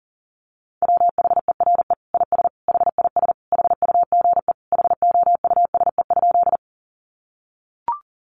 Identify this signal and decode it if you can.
Morse code, heard as a keyed tone, about 40 wpm, 700 Hz. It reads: WHERE IS HIS HUGE HOUSE?
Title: Sentences from Top 1100 Words 40wpm